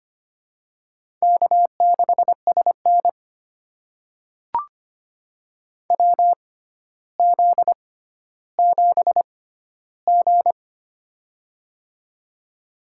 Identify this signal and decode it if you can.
Morse code, heard as a keyed tone, about 25 wpm, spaced as for 13 wpm, 700 Hz. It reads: K6SN WZ7G
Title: USA Call Signs - 4 Characters - 6x Character Spacing - 25wpm